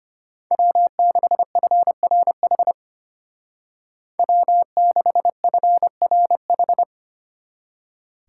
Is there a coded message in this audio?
Yes: Morse code from 0.5 to 6.8 s, heard as a keyed tone, about 33 wpm, 700 Hz. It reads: W6FRH W6FRH